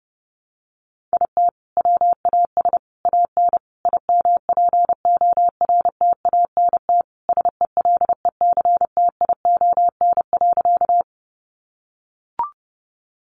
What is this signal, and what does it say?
Morse code, heard as a keyed tone, about 30 wpm, 700 Hz. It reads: IT WAS AN IMPORTANT SELECTION.
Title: Sentences from Top 1400 Words 30wpm